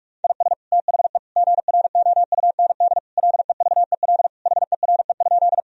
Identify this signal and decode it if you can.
Morse code, heard as a keyed tone, about 46 wpm, 700 Hz. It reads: IS THE GROUND LEVEL HERE?
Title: Sentences from Top 1000 Words - Review Part 4 - 45wpm